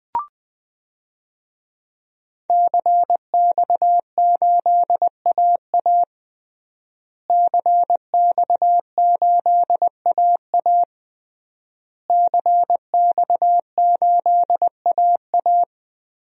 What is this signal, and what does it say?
Morse code, heard as a keyed tone, about 20 wpm, 700 Hz. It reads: CX8AA CX8AA CX8AA
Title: DXCC-100 Call Signs - 5 Characters - Repeated 3x - 20wpm